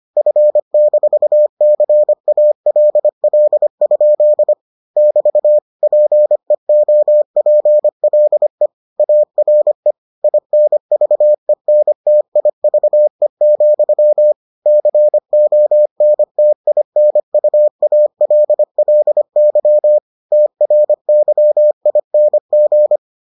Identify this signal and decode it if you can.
Morse code, heard as a keyed tone, about 25 wpm, 600 Hz. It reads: F-CALL? <BT> PEOPLE ARE INVENTIVE, CONTINUALLY TRYING